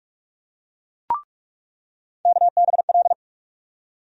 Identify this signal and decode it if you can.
Morse code, heard as a keyed tone, about 42 wpm, 700 Hz. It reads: KBL
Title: USA Call Sign Postfixes - 3 Letters 45wpm